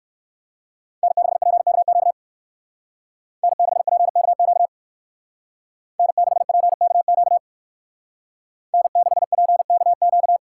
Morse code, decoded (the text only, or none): N6PKQ N6PKQ N6PKQ N6PKQ